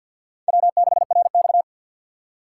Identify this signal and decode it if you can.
Morse code, heard as a keyed tone, about 46 wpm, 700 Hz. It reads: W6RX
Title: USA Call Signs - 4 Characters - 2x Character Spacing - 50wpm